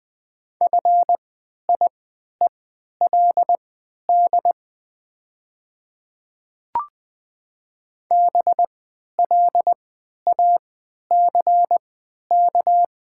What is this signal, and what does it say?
Morse code, heard as a keyed tone, about 20 wpm, spaced as for 11 wpm, 700 Hz. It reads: FIELD BLACK
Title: 3x Character Spacing - Top 500 Words - 20wpm